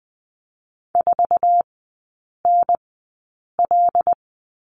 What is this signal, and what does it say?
Morse code, heard as a keyed tone, about 20 wpm, spaced as for 8 wpm, 690 Hz. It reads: 4NL
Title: 159 Lesson 26e - Letter g - Three Characters 20wpm